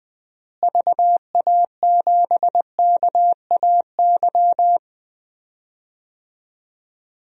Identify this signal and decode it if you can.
Morse code, heard as a keyed tone, about 20 wpm, 700 Hz. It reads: VA7KAY
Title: DXCC-100 Call Signs - 6 Characters - Repeated 2x - 20wpm